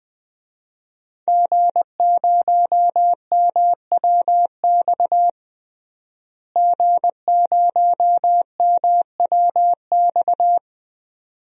Morse code, heard as a keyed tone, about 20 wpm, 700 Hz. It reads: G0MWX G0MWX
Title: DXCC-100 Call Signs - 5 Characters - Repeated 2x - 20wpm